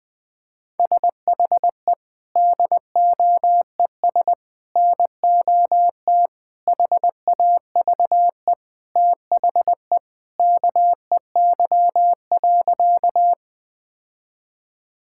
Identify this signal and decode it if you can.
Morse code, heard as a keyed tone, about 20 wpm, 700 Hz. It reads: SHE DOES NOT HAVE THE KEY.
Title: Sentences with 3-Word Phrases - 20wpm